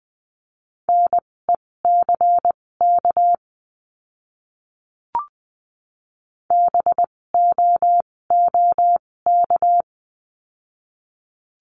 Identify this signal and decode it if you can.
Morse code, heard as a keyed tone, about 20 wpm, spaced as for 16 wpm, 700 Hz. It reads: NECK BOOK
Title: 256 Lesson 42a - Words 12-20wpm